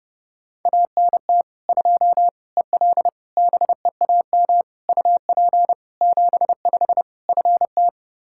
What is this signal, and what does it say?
Morse code, heard as a keyed tone, about 30 wpm, 700 Hz. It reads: ANT 2 EL BEAM UP 75 FT
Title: Practice QSO - Exchange 2 - Info - 30wpm